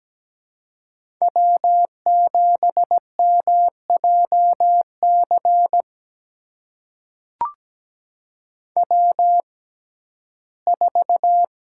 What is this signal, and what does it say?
Morse code, heard as a keyed tone, about 17 wpm, spaced as for 12 wpm, 700 Hz. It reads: W7MJC W4
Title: USA Call Signs - 5 Characters - 6x Character Spacing - 17wpm